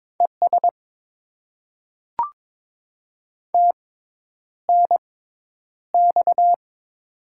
Morse code, heard as a keyed tone, about 21 wpm, spaced as for 10 wpm, 700 Hz. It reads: ES TNX
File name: Top 20 QSO Elements - 6x Character Spacing - 22wpm